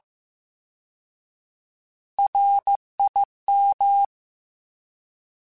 Morse code, heard as a keyed tone, about 15 wpm, 800 Hz. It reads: RIM